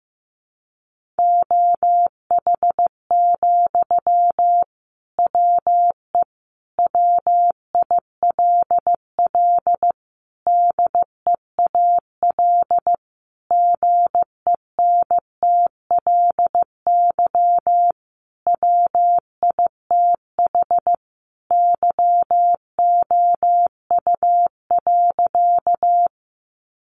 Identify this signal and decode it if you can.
Morse code, heard as a keyed tone, about 14 wpm, 700 Hz. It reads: OH, WE WILL DEAL GENTLY WITH YOU.
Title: Sentences from Top 1500 Words 15wpm